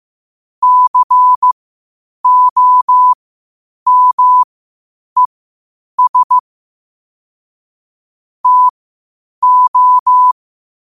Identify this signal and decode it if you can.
Morse code, heard as a keyed tone, about 15 wpm, spaced as for 8 wpm, 1000 Hz. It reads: COMES TO
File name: qotdfiles / 8wpm